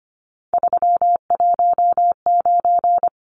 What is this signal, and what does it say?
Morse code, heard as a keyed tone, about 25 wpm, 700 Hz. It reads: 319
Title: Numbers - 3-Digit 25wpm